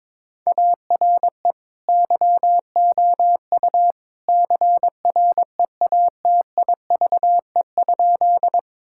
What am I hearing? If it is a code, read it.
Morse code, heard as a keyed tone, about 22 wpm, 700 Hz. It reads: ARE YOU CREATIVE?
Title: Sentences from Top 1000 Words - Review Part 5 - 22wpm